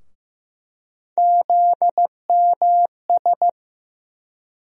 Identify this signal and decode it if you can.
Morse code, heard as a keyed tone, about 15 wpm, 700 Hz. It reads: ZMS
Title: USA Call Sign Postfixes - 3 Letters 15wpm